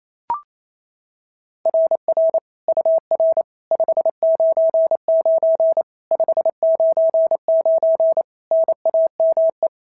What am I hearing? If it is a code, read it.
Morse code, heard as a keyed tone, about 27 wpm, 660 Hz. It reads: RR UR 599 599 NAME
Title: SOTA QSO Practice 28wpm